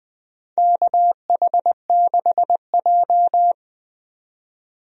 Morse code, heard as a keyed tone, about 20 wpm, 700 Hz. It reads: KH6J
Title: DXCC-100 Call Signs - 4 Characters - 20wpm